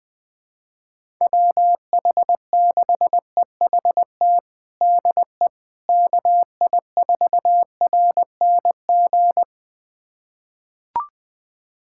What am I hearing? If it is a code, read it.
Morse code, heard as a keyed tone, about 21 wpm, 700 Hz. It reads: WH6EHT DE KI4RNG
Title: USA Callsign DE Callsign 20wpm